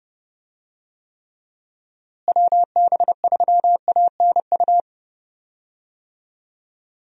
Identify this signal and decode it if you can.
Morse code, heard as a keyed tone, about 30 wpm, 700 Hz. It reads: WB3ANU
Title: USA Call Signs - 6 Characters - Repeated 2x - 30wpm